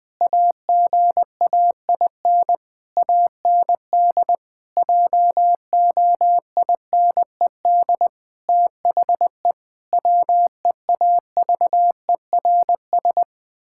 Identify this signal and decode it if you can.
Morse code, heard as a keyed tone, about 21 wpm, 700 Hz. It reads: AGAIN AND JOINED THE WEAVERS
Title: Pangram Challenge v2 - 20wpm